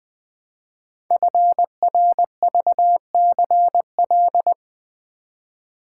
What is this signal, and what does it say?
Morse code, heard as a keyed tone, about 20 wpm, 700 Hz. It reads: FRVCL
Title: Code Groups - 5 Letters - 20wpm